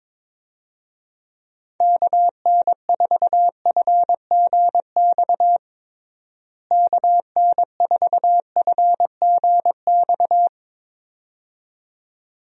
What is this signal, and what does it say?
Morse code, heard as a keyed tone, about 22 wpm, 700 Hz. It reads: KN4FGX KN4FGX
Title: USA Call Signs - 6 Characters - Repeated 2x - 22wpm